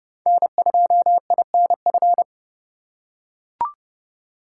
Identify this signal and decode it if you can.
Morse code, heard as a keyed tone, about 31 wpm, 700 Hz. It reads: N2INF
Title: USA Call Signs - 3x Character Spacing - 30wpm